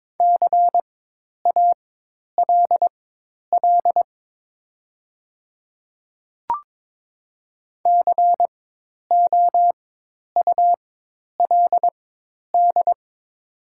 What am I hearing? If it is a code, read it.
Morse code, heard as a keyed tone, about 22 wpm, spaced as for 10 wpm, 700 Hz. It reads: CALL COULD
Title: 4x Character Spacing - Top 100 Words - 22wpm